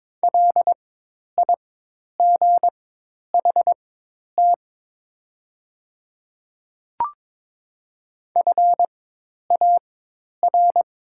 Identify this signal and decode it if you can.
Morse code, heard as a keyed tone, about 23 wpm, spaced as for 10 wpm, 700 Hz. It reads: LIGHT FAR
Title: 4x Character Spacing - Top 500 Words - 22wpm